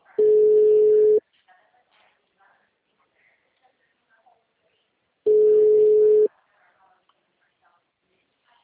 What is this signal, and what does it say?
Ringback tone (Europe, 425 Hz), with speech.
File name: out- -2021- - - .56721